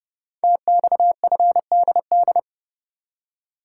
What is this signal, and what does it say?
Morse code, heard as a keyed tone, about 28 wpm, 700 Hz. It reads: TXFDD
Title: Code Groups - 5 Letters - 30wpm